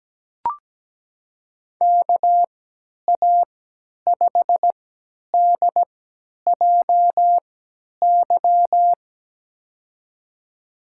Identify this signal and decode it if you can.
Morse code, heard as a keyed tone, about 17 wpm, spaced as for 10 wpm, 700 Hz. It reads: KA5DJY
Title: USA Call Signs - 6 Characters - 3x Character Spacing - 17wpm